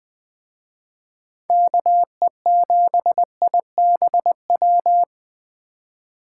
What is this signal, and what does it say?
Morse code, heard as a keyed tone, about 20 wpm, 700 Hz. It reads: KE7IBW